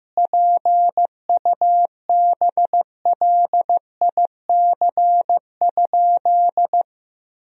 Morse code, heard as a keyed tone, about 15 wpm, 700 Hz. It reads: PUBLIC?
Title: Sentences from Top 1000 Words - Review Part 3 - 15wpm